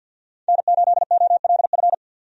Morse code, heard as a keyed tone, about 50 wpm, 700 Hz. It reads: N8OLF
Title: USA Call Signs - 5 Characters - 50wpm